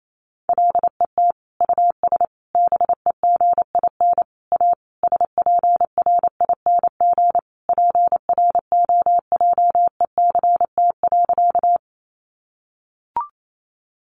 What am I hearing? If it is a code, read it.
Morse code, heard as a keyed tone, about 28 wpm, 700 Hz. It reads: LET US BEGIN A SPRING PROJECT.
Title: Sentences from Top 1400 Words 28wpm